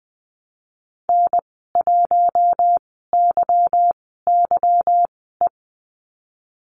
Morse code, heard as a keyed tone, about 20 wpm, spaced as for 15 wpm, 700 Hz. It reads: N1YYE
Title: 173 Lesson 28g - Number 9 - Call Signs 20wpm